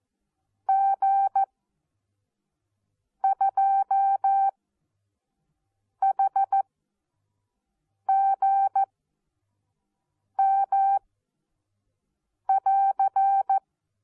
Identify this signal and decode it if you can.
Morse code, heard as a keyed tone, about 14 wpm, spaced as for 5 wpm, 780 Hz. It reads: G2HGM<AR>
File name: Onlinelehrgang / morsen / DJ4UF Morsen14